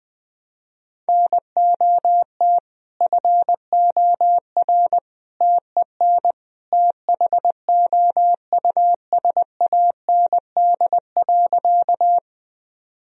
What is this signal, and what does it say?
Morse code, heard as a keyed tone, about 20 wpm, 700 Hz. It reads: NOT FOR TEN THOUSAND.
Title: Sentences from Top 1000 Words - Review Part 3 - 20wpm